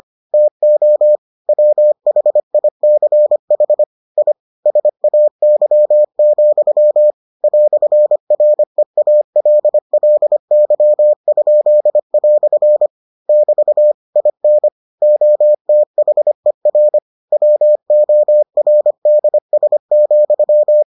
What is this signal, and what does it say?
Morse code, heard as a keyed tone, about 25 wpm, 600 Hz. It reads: TO WHICH I SAY, "REALLY?" <BT> IN OTHER WORDS,